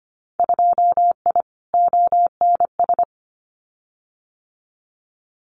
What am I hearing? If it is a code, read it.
Morse code, heard as a keyed tone, about 25 wpm, 700 Hz. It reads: 2I ONS
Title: Winter Field Day Exchanges Encore 25wpm